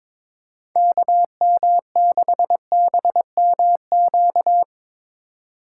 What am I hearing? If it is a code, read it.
Morse code, heard as a keyed tone, about 22 wpm, 700 Hz. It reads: KM6BMQ
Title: USA Call Signs - 6 Characters - 2x Character Spacing - 22wpm